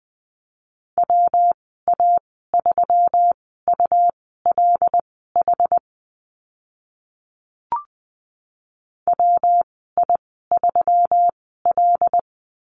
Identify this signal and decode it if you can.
Morse code, heard as a keyed tone, about 20 wpm, spaced as for 14 wpm, 700 Hz. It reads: WA3ULH WI3L
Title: 128 Lesson 21g - Number 6 - Call Signs 20wpm